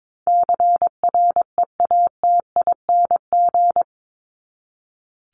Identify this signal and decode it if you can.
Morse code, heard as a keyed tone, about 22 wpm, 700 Hz. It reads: CREATING
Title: ING Suffix - Top 100 Words with Base Words 22wpm